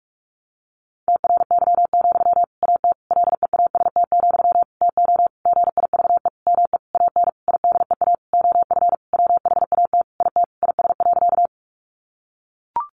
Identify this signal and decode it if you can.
Morse code, heard as a keyed tone, about 44 wpm, 700 Hz. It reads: TRY, AT LEAST, TO GIVE ME AN IDEA OF WHAT IT IS.
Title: Sentences from Top 700 Words 45wpm